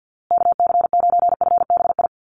Morse code, heard as a keyed tone, about 50 wpm, 700 Hz. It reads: KC9FBI
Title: USA Call Signs Repeated 2x 50wpm